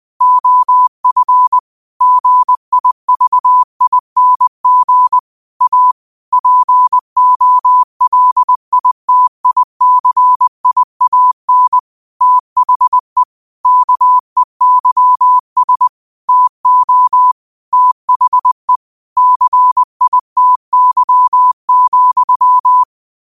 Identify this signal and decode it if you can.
Morse code, heard as a keyed tone, about 20 wpm, 1000 Hz. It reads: OF GIVING A POLITICIAN THE KEYS TO THE CITY,